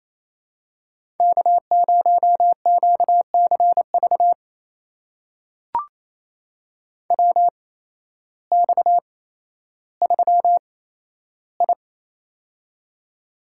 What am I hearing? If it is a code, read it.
Morse code, heard as a keyed tone, about 28 wpm, spaced as for 13 wpm, 700 Hz. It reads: K0QCV WX3I
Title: USA Call Signs - 8x Character Spacing - 28wpm